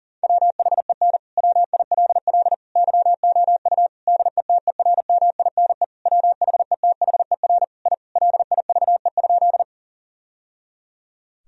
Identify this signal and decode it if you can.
Morse code, heard as a keyed tone, about 37 wpm, 700 Hz. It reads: WHEN WILL YOU DETERMINE WHETHER I LIVE?
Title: Sentences from Top 1000 Words - Review Part 4 - 40wpm